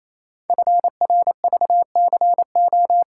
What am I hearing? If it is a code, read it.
Morse code, heard as a keyed tone, about 28 wpm, 700 Hz. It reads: FRVCO
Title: Code Groups - 5 Letters - 28wpm